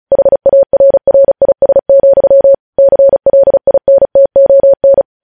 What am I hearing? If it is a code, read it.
Morse code, heard as a keyed tone, about 36 wpm, 550 Hz. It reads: HARRIS, CLINTON
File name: podcast / News headlines-1x-35x35